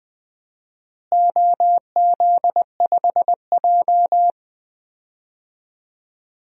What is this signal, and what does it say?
Morse code, heard as a keyed tone, about 20 wpm, 700 Hz. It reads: OZ5J
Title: DXCC-100 Call Signs - 4 Characters - Repeated 2x - 20wpm